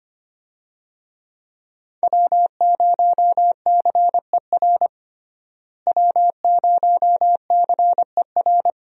Morse code, heard as a keyed tone, about 25 wpm, 700 Hz. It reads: W0CER W0CER
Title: USA Call Signs - 5 Characters - Repeated 3x - 25wpm